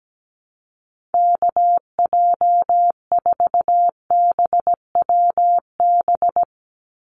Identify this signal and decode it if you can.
Morse code, heard as a keyed tone, about 17 wpm, 700 Hz. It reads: KJ4BWB
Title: USA Call Signs 17wpm